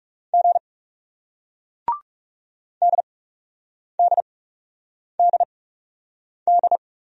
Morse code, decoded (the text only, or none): G DDDD